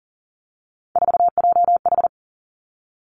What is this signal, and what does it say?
Morse code, heard as a keyed tone, about 42 wpm, 700 Hz. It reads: 4JH
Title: Three Challenging Letter-Number Combinations 40wpm